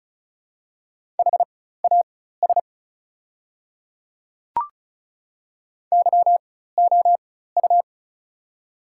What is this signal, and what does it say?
Morse code, heard as a keyed tone, about 34 wpm, spaced as for 16 wpm, 700 Hz. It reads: HAS YOU
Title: 4x Character Spacing - Top 100 Words - 35wpm